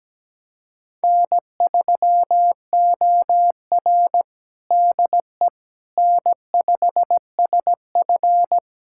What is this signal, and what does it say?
Morse code, heard as a keyed tone, about 17 wpm, 700 Hz. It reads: N3OR DE N5SF
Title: USA Callsign DE Callsign - 4 Characters - 17wpm